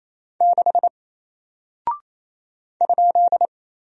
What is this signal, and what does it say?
Morse code, heard as a keyed tone, about 27 wpm, 700 Hz. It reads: 6 ?
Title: Single Character (Letter:Number:Punctuation:Prosign) Alt 28wpm